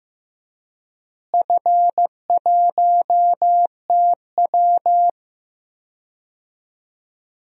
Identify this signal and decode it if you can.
Morse code, heard as a keyed tone, about 15 wpm, 700 Hz. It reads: F1TW
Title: DXCC-100 Call Signs - 4 Characters - Repeated 3x - 15wpm